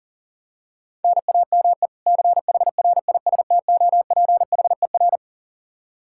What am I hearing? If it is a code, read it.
Morse code, heard as a keyed tone, about 40 wpm, 700 Hz. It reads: NAME CHRISTOPHER
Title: Practice QSO - Exchange 1 - Beginning - 40wpm